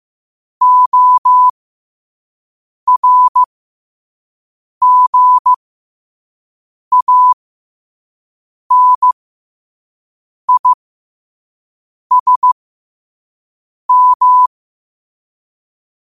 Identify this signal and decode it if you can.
Morse code, heard as a keyed tone, about 15 wpm, spaced as for 5 wpm, 1000 Hz. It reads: ORGANISM
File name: qotdfiles / 5wpm